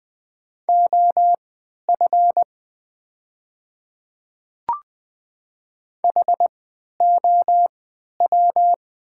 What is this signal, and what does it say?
Morse code, heard as a keyed tone, about 20 wpm, spaced as for 11 wpm, 700 Hz. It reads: OF HOW
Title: 3x Character Spacing - Top 100 Words - 20wpm